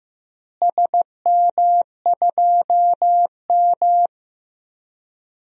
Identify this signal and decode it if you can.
Morse code, heard as a keyed tone, about 15 wpm, 700 Hz. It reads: SM2M
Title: DXCC-100 Call Signs - 4 Characters - Repeated 2x - 15wpm